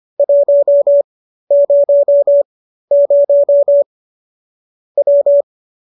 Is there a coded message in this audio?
Yes: Morse code from 0.2 to 5.4 s, heard as a keyed tone, about 25 wpm, spaced as for 13 wpm, 570 Hz. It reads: 100 W